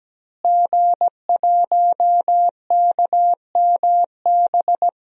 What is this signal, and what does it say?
Morse code, heard as a keyed tone, about 17 wpm, 700 Hz. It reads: G1KMB
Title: DXCC-100 Call Signs - 5 Characters - Repeated 2x - 17wpm